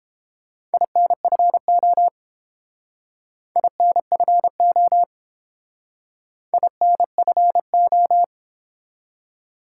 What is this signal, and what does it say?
Morse code, heard as a keyed tone, about 29 wpm, 700 Hz. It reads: INFO INFO INFO